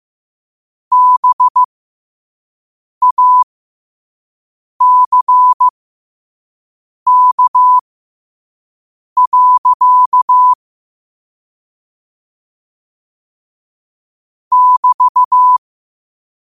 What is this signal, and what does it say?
Morse code, heard as a keyed tone, about 15 wpm, spaced as for 5 wpm, 1000 Hz. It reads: BACK. <BT>
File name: qotdfiles / 5wpm